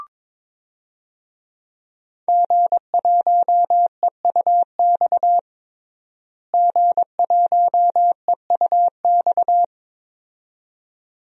Morse code, heard as a keyed tone, about 22 wpm, 700 Hz. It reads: G1EUX G1EUX
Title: DXCC-100 Call Signs - 5 Characters - Repeated 2x - 22wpm